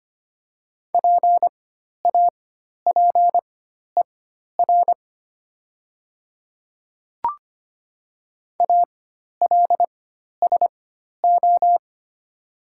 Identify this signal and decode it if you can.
Morse code, heard as a keyed tone, about 25 wpm, spaced as for 12 wpm, 700 Hz. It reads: PAPER ALSO
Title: 4x Character Spacing - Top 300 Words - 25wpm